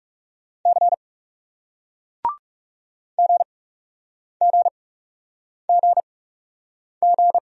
Morse code, heard as a keyed tone, about 36 wpm, spaced as for 8 wpm, 700 Hz. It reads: C GGGG
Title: Speed-Racing Single Letters 30wpm